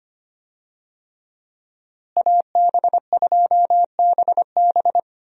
Morse code, heard as a keyed tone, about 25 wpm, 700 Hz. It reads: AB2BB
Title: USA Call Signs - 5 Characters - Repeated 3x - 25wpm